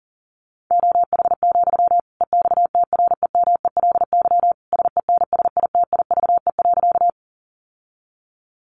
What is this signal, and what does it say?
Morse code, heard as a keyed tone, about 40 wpm, 700 Hz. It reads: OH, EXTREMELY SENSITIVE.
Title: Sentences from Top 1400 Words 40wpm